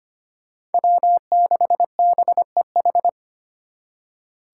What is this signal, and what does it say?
Morse code, heard as a keyed tone, about 33 wpm, 700 Hz. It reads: W6BEH